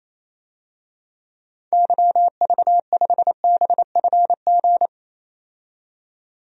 Morse code, heard as a keyed tone, about 27 wpm, 700 Hz. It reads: YV5BFG
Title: DXCC-100 Call Signs - 6 Characters - 28wpm